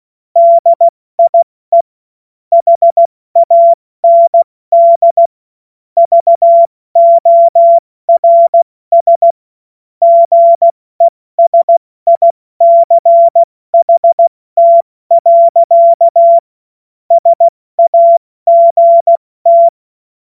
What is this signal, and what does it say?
Morse code, heard as a keyed tone, about 16 wpm, 680 Hz. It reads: DIE HAND VORS GESICHT. SAGT
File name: Morsekurs / Witze / Witz 05 März 2024 70 F80